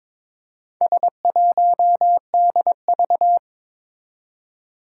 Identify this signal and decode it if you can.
Morse code, heard as a keyed tone, about 21 wpm, 700 Hz. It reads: S1DV
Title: Four Challenging Letter-Number Combinations 22wpm